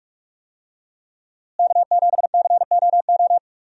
Morse code, heard as a keyed tone, about 41 wpm, 700 Hz. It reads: K7COO
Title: USA Call Signs - 5 Characters - Repeated 3x - 45wpm